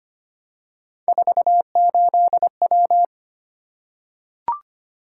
Morse code, heard as a keyed tone, about 25 wpm, 700 Hz. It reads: 48W
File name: Three Letter-Number Combinations - 25wpm